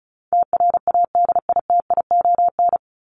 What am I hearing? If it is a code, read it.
Morse code, heard as a keyed tone, about 34 wpm, 700 Hz. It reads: TRADITION
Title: Top 1251-1500 Words 35wpm